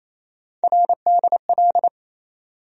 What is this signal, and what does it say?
Morse code, heard as a keyed tone, about 28 wpm, 700 Hz. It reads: RDL
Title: USA Call Sign Postfixes - 3 Letters 28wpm